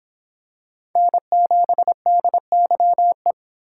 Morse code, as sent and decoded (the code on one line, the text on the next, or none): -. --... -.. -.-- .
N7DYE